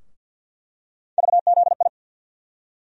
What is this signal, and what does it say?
Morse code, heard as a keyed tone, about 47 wpm, 700 Hz. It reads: VZI
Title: USA Call Sign Postfixes - 3 Letters 50wpm